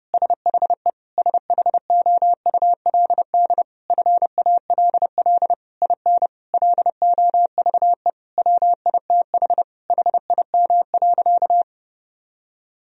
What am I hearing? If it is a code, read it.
Morse code, heard as a keyed tone, about 30 wpm, 700 Hz. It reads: SHE SHOULD FALL IN LOVE WITH HIM.
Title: Sentences from Top 1000 Words - Review Part 4 - 30wpm